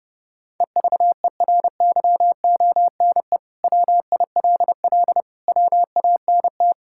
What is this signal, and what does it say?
Morse code, heard as a keyed tone, about 30 wpm, 700 Hz. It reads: EVERYONE WILL WANT
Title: Sentences from Top 1000 Words - Review Part 5 - 30wpm